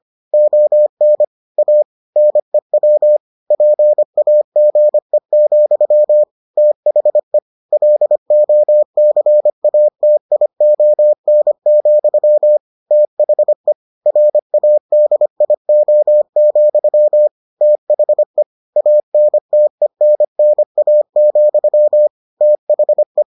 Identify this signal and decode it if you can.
Morse code, heard as a keyed tone, about 25 wpm, 600 Hz. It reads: ON A NEW PAGE, THE LOCATION, THE RADIO, THE ANTENNA, THE